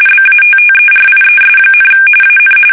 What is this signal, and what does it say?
RTTY, about 45.45 baud. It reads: BATTERIES, AN
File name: fsk850 / USS-Massachusetts